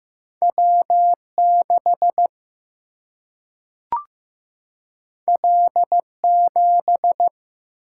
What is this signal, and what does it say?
Morse code, heard as a keyed tone, about 15 wpm, 700 Hz. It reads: W 6 L 7
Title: Single-Letter Single-Number 15wpm